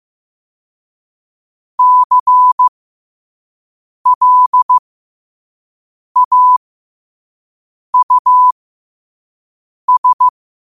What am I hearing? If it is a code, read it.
Morse code, heard as a keyed tone, about 15 wpm, spaced as for 5 wpm, 1000 Hz. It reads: CLAUS
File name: qotdfiles / 5wpm